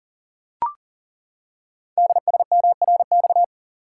Morse code, heard as a keyed tone, about 39 wpm, 690 Hz. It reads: DSMRX
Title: Code Groups - 5 Letters - 40wpm